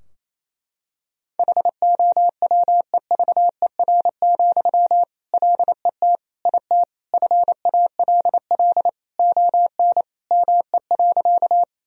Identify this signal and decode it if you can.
Morse code, heard as a keyed tone, about 27 wpm, 700 Hz. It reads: HOWEVER, LET IT FALL ON ME.
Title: Sentences from Top 1000 Words - Review Part 2 - 28wpm